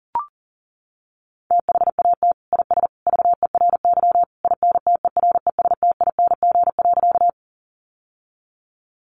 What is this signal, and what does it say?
Morse code, heard as a keyed tone, about 38 wpm, 700 Hz. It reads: THAT IS VERY INTERESTING.
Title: Sentences from Top 1000 Words 40wpm